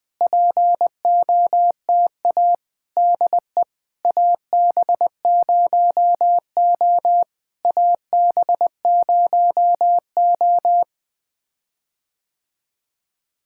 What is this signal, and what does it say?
Morse code, heard as a keyed tone, about 20 wpm, 700 Hz. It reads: POTA DE AB0O AB0O
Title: POTA QSO Practice - 20wpm